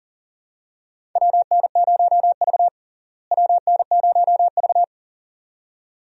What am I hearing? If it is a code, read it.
Morse code, heard as a keyed tone, about 40 wpm, 700 Hz. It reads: WN0V WN0V
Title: USA Call Signs - 4 Characters - Repeated 2x - 40wpm